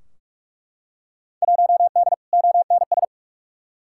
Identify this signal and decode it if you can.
Morse code, heard as a keyed tone, about 46 wpm, 700 Hz. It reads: 1D ONS
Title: Field Day Exchanges - 45wpm